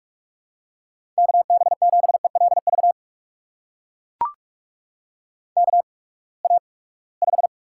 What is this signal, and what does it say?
Morse code, heard as a keyed tone, about 43 wpm, spaced as for 25 wpm, 700 Hz. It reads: KB7ELV KA5
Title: USA Call Signs - 6 Characters - 8x Character Spacing - 45wpm